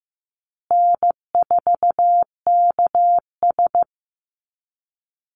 Morse code, heard as a keyed tone, about 15 wpm, 700 Hz. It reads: N4KS
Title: Easy Call Signs Repeated 3x - 15wpm